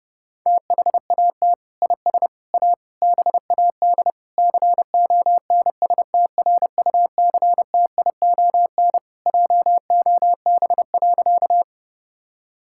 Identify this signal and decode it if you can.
Morse code, heard as a keyed tone, about 32 wpm, 700 Hz. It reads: THAT IS A BAD CONSTRUCTION JOB.
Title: Sentences from Top 1600 Words 30wpm